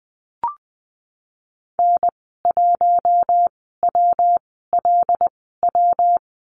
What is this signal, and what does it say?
Morse code, heard as a keyed tone, about 20 wpm, spaced as for 14 wpm, 700 Hz. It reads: N1WLW
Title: 147 Lesson 24g - Letter y - Call Signs 20wpm